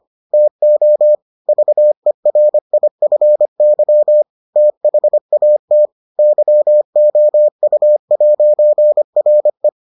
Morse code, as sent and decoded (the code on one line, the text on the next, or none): - --- / ...- . .-. .. ..-. -.-- / - .... .- - / -.-- --- ..- .----. .-. .
TO VERIFY THAT YOU'RE